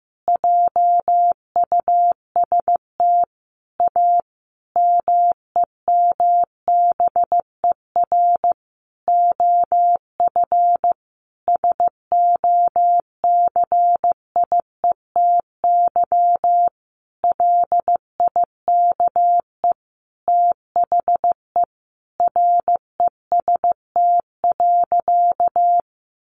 Morse code, heard as a keyed tone, about 15 wpm, 700 Hz. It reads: JUST A MEMBER OF SOCIETY LIKE THE REST.
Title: Sentences from Top 1100 Words 15wpm